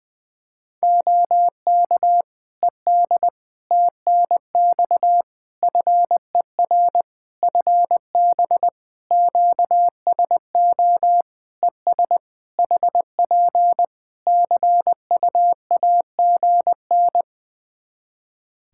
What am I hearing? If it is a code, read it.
Morse code, heard as a keyed tone, about 19 wpm, 700 Hz. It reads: OK ED TNX FER FB QSO ES HP CUAGN
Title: Practice QSO - Exchange 4 - Ending - 20wpm